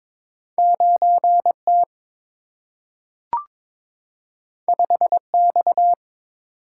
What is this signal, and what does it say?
Morse code, heard as a keyed tone, about 22 wpm, 700 Hz. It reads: 9 T 5 X
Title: Single-Letter Single-Number 22wpm